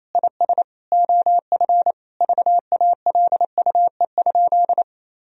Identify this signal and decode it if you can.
Morse code, heard as a keyed tone, about 27 wpm, 690 Hz. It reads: IS OF VALUE?
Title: Sentences from Top 1000 Words - Review Part 3 - 28wpm